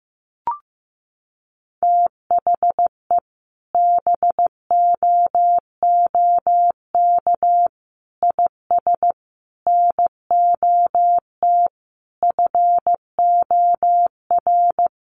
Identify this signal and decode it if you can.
Morse code, heard as a keyed tone, about 15 wpm, 700 Hz. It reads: THE BOOK IS NOT FOR
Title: Sentences from Top 1400 Words 15wpm